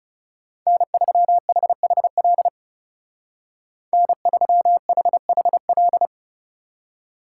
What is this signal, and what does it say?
Morse code, heard as a keyed tone, about 35 wpm, 700 Hz. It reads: N3HHL N3HHL
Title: Speed Racing - USA 1x3 Call Signs 30wpm